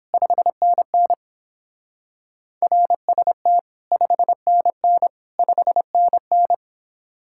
Morse code, heard as a keyed tone, about 31 wpm, 700 Hz. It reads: 5NN RST 5NN 5NN